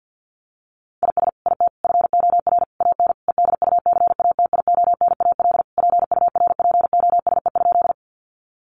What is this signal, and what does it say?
Morse code, heard as a keyed tone, about 50 wpm, 700 Hz. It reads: IS IT FOR AN EDUCATIONAL PURPOSE?
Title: Sentences from Top 1200 Words 50wpm